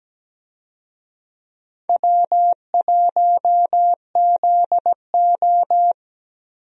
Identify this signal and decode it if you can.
Morse code, heard as a keyed tone, about 17 wpm, 700 Hz. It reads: W1ZO
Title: USA Call Signs - 4 Characters - Repeated 2x - 17wpm